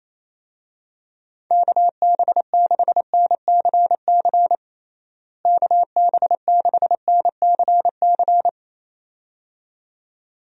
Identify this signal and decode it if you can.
Morse code, heard as a keyed tone, about 28 wpm, 700 Hz. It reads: KB6NCC KB6NCC
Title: USA Call Signs - 6 Characters - Repeated 2x - 28wpm